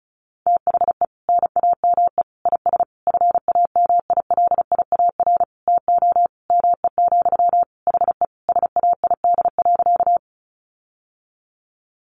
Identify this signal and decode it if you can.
Morse code, heard as a keyed tone, about 34 wpm, 700 Hz. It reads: THE NAME IS FAMILIAR TO ME, HE SAID.
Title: Sentences from Top 1000 Words 35wpm